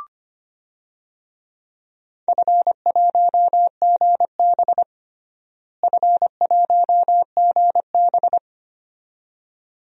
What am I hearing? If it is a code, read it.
Morse code, heard as a keyed tone, about 25 wpm, 700 Hz. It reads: F1GB F1GB
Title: DXCC-100 Call Signs - 4 Characters - Repeated 2x - 25wpm